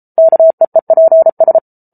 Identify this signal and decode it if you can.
Morse code, heard as a keyed tone, about 34 wpm, 650 Hz. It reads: KEEPS